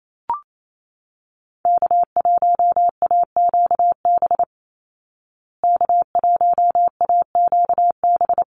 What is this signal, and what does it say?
Morse code, heard as a keyed tone, about 28 wpm, 700 Hz. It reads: K1AQB K1AQB
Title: Call Signs Repeated 3x - 28wpm